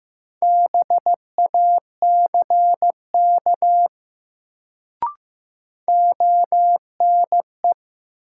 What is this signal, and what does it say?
Morse code, heard as a keyed tone, about 15 wpm, 700 Hz. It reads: BACK ONE
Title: Sets of 2 Words - New Word Drill - 15wpm